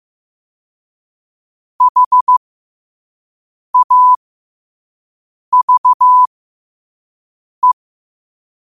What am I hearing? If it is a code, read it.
Morse code, heard as a keyed tone, about 15 wpm, spaced as for 5 wpm, 1000 Hz. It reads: HAVE
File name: qotdfiles / 5wpm